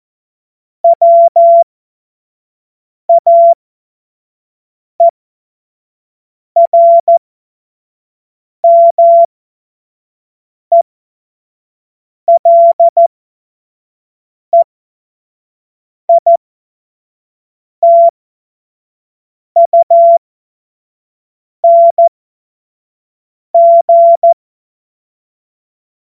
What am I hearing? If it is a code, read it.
Morse code, heard as a keyed tone, about 14 wpm, spaced as for 5 wpm, 680 Hz. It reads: WAERMELEITUNG
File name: Morsekurs / Witze / Witz 17. Mai 2022 25 F70